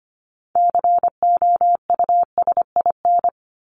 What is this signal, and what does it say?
Morse code, heard as a keyed tone, about 24 wpm, 700 Hz. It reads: COUSIN
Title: Top 1501-1750 Words 25wpm